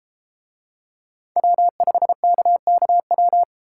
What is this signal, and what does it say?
Morse code, heard as a keyed tone, about 31 wpm, 700 Hz. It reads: W5KKW